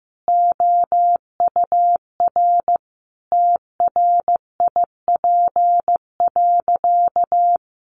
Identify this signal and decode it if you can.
Morse code, heard as a keyed tone, about 15 wpm, 700 Hz. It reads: OUR TRIP.
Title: Sentences from Top 1300 Words 15wpm